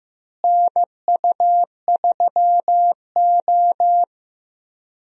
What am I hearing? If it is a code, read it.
Morse code, heard as a keyed tone, about 15 wpm, 700 Hz. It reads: NU3O
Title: USA Call Signs - 4 Characters - Repeated 3x - 15wpm